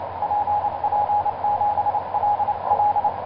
Morse code, heard as a keyed tone, about 26 wpm, 830 Hz, with radio noise. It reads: PP7LL
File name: zp6cw ik1rgk etc 12m cw 19feb2010 1640